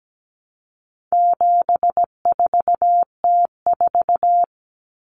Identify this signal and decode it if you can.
Morse code, heard as a keyed tone, about 17 wpm, 700 Hz. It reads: 74T4
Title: Common Cut Numbers - 4-Digit 17wpm